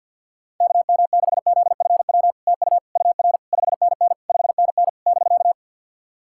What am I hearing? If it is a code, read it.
Morse code, heard as a keyed tone, about 48 wpm, 700 Hz. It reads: KM6ZFW TU UR 5NN 5NN <BK>
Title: SOTA QSO Practice 50wpm